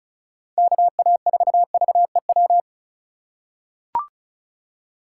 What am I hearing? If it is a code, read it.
Morse code, heard as a keyed tone, about 36 wpm, 700 Hz. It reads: KA4VEW
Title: USA Call Signs - 6 Characters - 4x Character Spacing - 35wpm